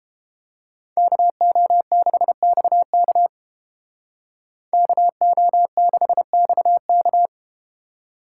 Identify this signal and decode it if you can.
Morse code, heard as a keyed tone, about 30 wpm, 700 Hz. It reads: KO6XK KO6XK